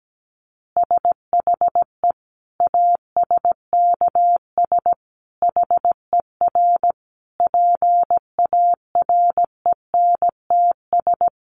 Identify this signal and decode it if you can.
Morse code, heard as a keyed tone, about 17 wpm, 700 Hz. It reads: SHE ASKS HER PARENTS
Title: Easy Phrases 17wpm